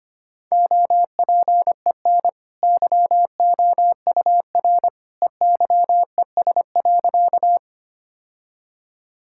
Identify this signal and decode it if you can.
Morse code, heard as a keyed tone, about 25 wpm, 690 Hz. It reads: OPEN YOUR EYES.
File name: Easy Sentences with Hints - 25wpm